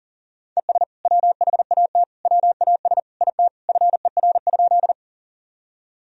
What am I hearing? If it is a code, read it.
Morse code, heard as a keyed tone, about 40 wpm, 700 Hz. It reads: ES WHAT WAS IT FER?
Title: Sentences from Top 100 Words and Abbreviations 40wpm